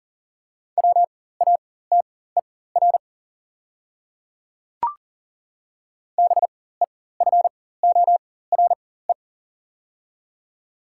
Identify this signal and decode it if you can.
Morse code, heard as a keyed tone, about 40 wpm, spaced as for 19 wpm, 700 Hz. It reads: WATER BEFORE
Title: 4x Character Spacing - Top 200 Words - 40wpm